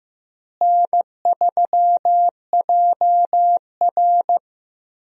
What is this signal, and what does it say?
Morse code, heard as a keyed tone, about 15 wpm, 700 Hz. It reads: N3JR
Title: USA Call Signs - 4 Characters - Repeated 3x - 15wpm